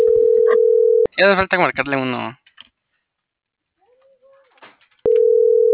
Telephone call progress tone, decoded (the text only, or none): ringback tone